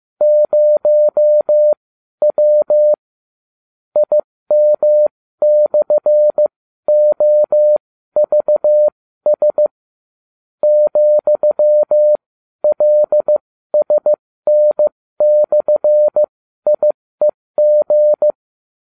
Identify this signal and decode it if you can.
Morse code, heard as a keyed tone, about 15 wpm, spaced as for 12 wpm, 600 Hz. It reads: 0W IM/OVS ,LSN/IEG